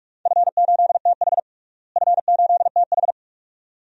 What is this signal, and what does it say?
Morse code, heard as a keyed tone, about 47 wpm, 700 Hz. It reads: F8TH F8TH
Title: DXCC-100 Call Signs - 4 Characters - Repeated 2x - 45wpm